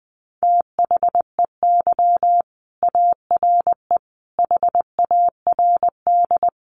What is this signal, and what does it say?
Morse code, heard as a keyed tone, about 20 wpm, 700 Hz. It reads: THEY ARE HARD
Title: Sentences from Top 300 Words 20wpm